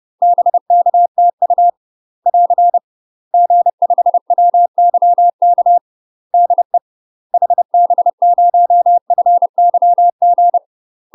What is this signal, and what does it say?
Morse code, heard as a keyed tone, about 30 wpm, 700 Hz. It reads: BK TU <AR> G5WYK DE HB0FYG
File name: UK MASTER QSOs TRAINING / 30WPM / UK MASTER QSOs 0020